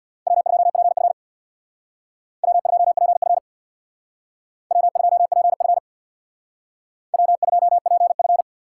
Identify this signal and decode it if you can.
Morse code, heard as a keyed tone, about 61 wpm, 700 Hz. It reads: W2PF W2PF W2PF W2PF
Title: Speed Racing - USA Call Signs 50wpm